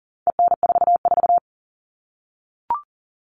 Morse code, heard as a keyed tone, about 40 wpm, 700 Hz. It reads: EN44
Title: All Cut Numbers - 4-Digit 40wpm